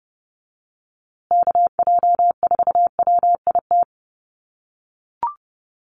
Morse code, heard as a keyed tone, about 30 wpm, 700 Hz. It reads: KJ4WIT
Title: USA Call Signs Repeated 3x 30wpm